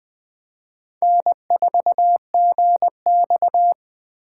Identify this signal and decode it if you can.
Morse code, heard as a keyed tone, about 20 wpm, 700 Hz. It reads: N4GX
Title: Speed Racing - USA 1x2 Call Signs 17wpm